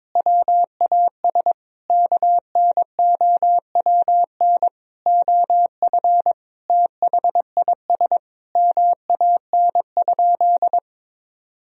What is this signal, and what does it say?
Morse code, heard as a keyed tone, about 22 wpm, 700 Hz. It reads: WAS KNOWN OF THIS MAN?
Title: Sentences from Top 1000 Words - Review Part 3 - 22wpm